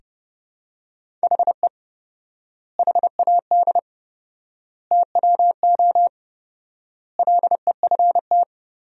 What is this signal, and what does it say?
Morse code, heard as a keyed tone, about 30 wpm, 700 Hz. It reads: HE HAD TWO LEFT
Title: 4x Word-Spacing - Sentences from Top 300 Words - 30wpm